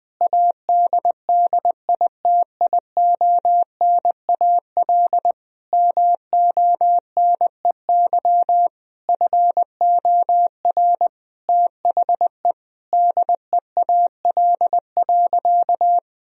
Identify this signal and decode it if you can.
Morse code, heard as a keyed tone, about 21 wpm, 700 Hz. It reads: ADDITIONAL MONEY FOR THE DEAL.
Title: Sentences from Top 1000 Words - Review Part 4 - 20wpm